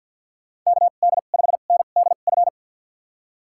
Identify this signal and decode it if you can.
Morse code, heard as a keyed tone, about 51 wpm, spaced as for 36 wpm, 700 Hz. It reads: KD5NDF
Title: USA Call Signs - 2x Character Spacing - 50wpm